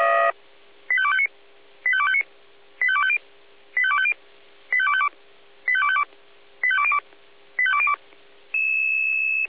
Five-tone selective call (ZVEI), heard as quick runs of five tones, tones of about 70 ms. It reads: 85259 85259 85250 85250 85252 85252 85202 85202